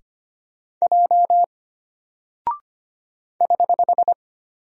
Morse code, heard as a keyed tone, about 25 wpm, 700 Hz. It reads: J <HH>
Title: Single Character (Letter:Number:Punctuation:Prosign) Alt 25wpm